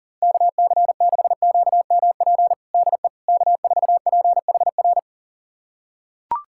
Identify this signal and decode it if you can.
Morse code, heard as a keyed tone, about 40 wpm, 700 Hz. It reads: KC6QMP DE K4PHR
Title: USA Callsign DE Callsign 40wpm